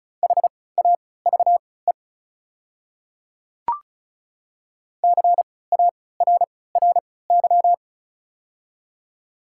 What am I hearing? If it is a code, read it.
Morse code, heard as a keyed tone, about 36 wpm, spaced as for 20 wpm, 690 Hz. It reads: HAVE CARRY
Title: 3x Character Spacing - Top 500 Words - 35wpm